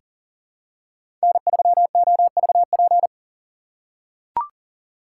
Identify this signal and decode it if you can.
Morse code, heard as a keyed tone, about 40 wpm, 700 Hz. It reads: N3OVP
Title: USA Call Signs - 5 Characters - 2x Character Spacing - 40wpm